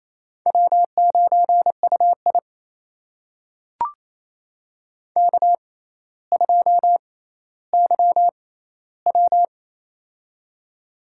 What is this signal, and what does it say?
Morse code, heard as a keyed tone, about 28 wpm, spaced as for 14 wpm, 700 Hz. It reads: W9UI K2YW
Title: USA Call Signs - 4 Characters - 6x Character Spacing - 28wpm